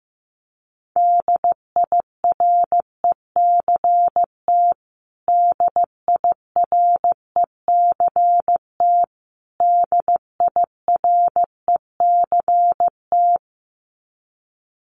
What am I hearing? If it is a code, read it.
Morse code, heard as a keyed tone, about 15 wpm, 700 Hz. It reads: DIRECT DIRECT DIRECT
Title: Top 500 Words Repeated 15wpm